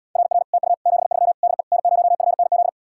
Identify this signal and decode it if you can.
Morse code, heard as a keyed tone, about 77 wpm, 700 Hz. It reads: RR TU 73 DE N0FNZ